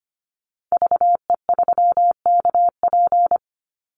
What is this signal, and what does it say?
Morse code, heard as a keyed tone, about 25 wpm, 700 Hz. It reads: VE3KP
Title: Call Signs - 25wpm